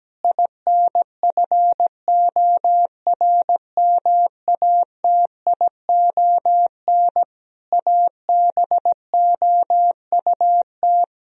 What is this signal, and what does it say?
Morse code, heard as a keyed tone, about 17 wpm, 700 Hz. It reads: INFORMATION ABOUT
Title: Sentences from Top 1000 Words - Review Part 3 - 17wpm